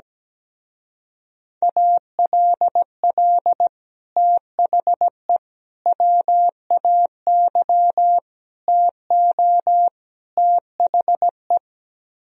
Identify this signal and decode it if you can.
Morse code, heard as a keyed tone, about 17 wpm, 700 Hz. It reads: ALL THE WAY TO THE
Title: Speed-Racing - Sets of 5 Words 17wpm